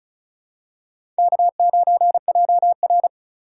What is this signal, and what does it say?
Morse code, heard as a keyed tone, about 34 wpm, 700 Hz. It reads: K9JR
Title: USA Call Signs - 4 Characters - 4x Character Spacing - 35wpm